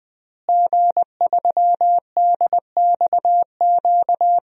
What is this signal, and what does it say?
Morse code, heard as a keyed tone, about 20 wpm, 700 Hz. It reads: G3DXQ
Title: DXCC-100 Call Signs - 5 Characters - Repeated 3x - 20wpm